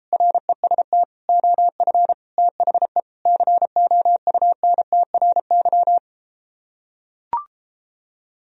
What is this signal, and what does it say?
Morse code, heard as a keyed tone, about 34 wpm, 700 Hz. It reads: REST OF THE COUNTRY